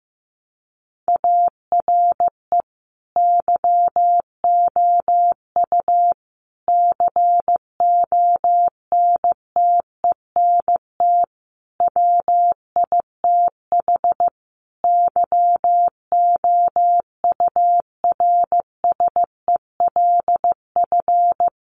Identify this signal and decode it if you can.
Morse code, heard as a keyed tone, about 15 wpm, 700 Hz. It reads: ARE YOU CONTENT WITH YOURSELF
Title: Sentences from Top 900 Words 15wpm